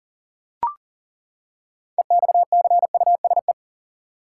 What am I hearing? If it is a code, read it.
Morse code, heard as a keyed tone, about 40 wpm, 700 Hz. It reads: EXCUSE
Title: Six Letter Words - 40wpm